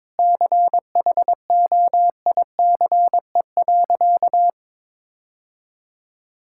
Morse code, heard as a keyed tone, about 22 wpm, 700 Hz. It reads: CHOICE.
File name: practice / 22wpm / Sentences with 3-Word Phrases - 22wpm